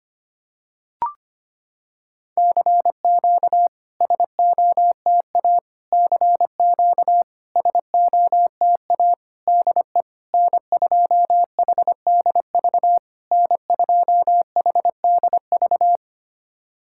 Morse code, heard as a keyed tone, about 25 wpm, 700 Hz. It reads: CQ SOTA CQ SOTA DE N2HDV N2HDV
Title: SOTA QSO Practice 25wpm